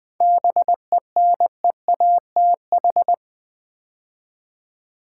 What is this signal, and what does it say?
Morse code, heard as a keyed tone, about 20 wpm, 700 Hz. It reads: BENEATH
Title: 7 Letter Words - First 3 Letters Given - 20wpm